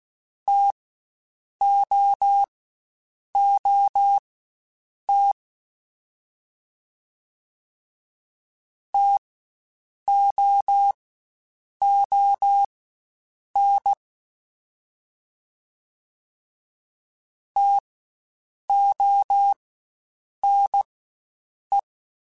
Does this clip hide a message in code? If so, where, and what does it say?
Yes: Morse code from 0.5 to 21.8 s, heard as a keyed tone, about 16 wpm, spaced as for 7 wpm, 800 Hz. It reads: TOOT TOON TONE